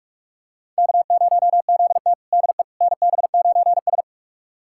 Morse code, heard as a keyed tone, about 44 wpm, 690 Hz. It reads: K0ZT DE NB9S
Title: USA Callsign DE Callsign - 4 Characters - 45wpm